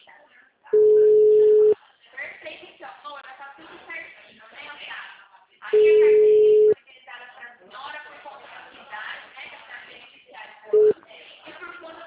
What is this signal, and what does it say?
Ringback tone (Europe, 425 Hz), with speech.